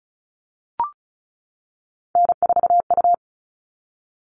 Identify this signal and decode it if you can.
Morse code, heard as a keyed tone, about 34 wpm, 700 Hz. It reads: N4U
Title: All Cut Numbers - 3-Digit 35wpm